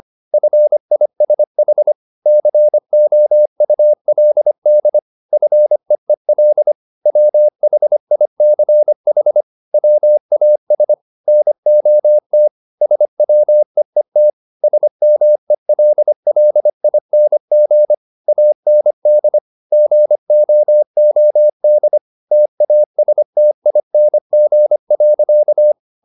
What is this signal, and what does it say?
Morse code, heard as a keyed tone, about 25 wpm, 600 Hz. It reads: FISH COULD FEEL WHICH WAS NOT SWEET SMELLING AND GOOD TASTING.
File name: the old man and the sea part 1 25wpm